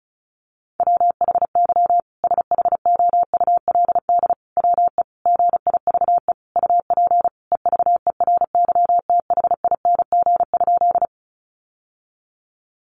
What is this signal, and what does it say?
Morse code, heard as a keyed tone, about 34 wpm, 700 Hz. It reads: WHY SHOULD WE GIVE UP EVERYTHING?
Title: Sentences from Top 700 Words 35wpm